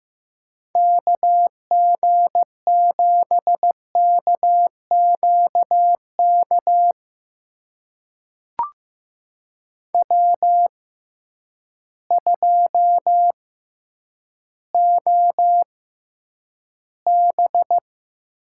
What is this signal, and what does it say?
Morse code, heard as a keyed tone, about 15 wpm, spaced as for 7 wpm, 700 Hz. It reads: KG7KQ KW2OB
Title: USA Call Signs - 6x Character Spacing - 15wpm